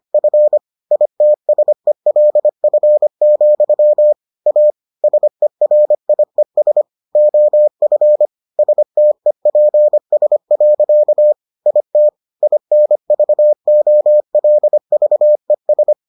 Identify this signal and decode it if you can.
Morse code, heard as a keyed tone, about 25 wpm, 600 Hz. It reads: F ITSELF, A SERIES OF STEPS. IT INVOLVES